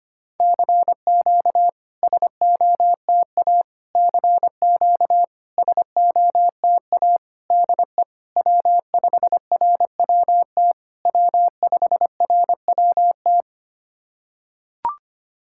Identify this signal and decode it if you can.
Morse code, heard as a keyed tone, about 25 wpm, 700 Hz. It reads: CQ SOTA CQ SOTA DE W5RWT W5RWT
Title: SOTA QSO Practice 25wpm